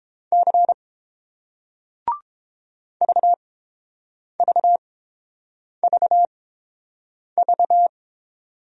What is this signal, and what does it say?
Morse code, heard as a keyed tone, about 30 wpm, spaced as for 7 wpm, 700 Hz. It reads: C VVVV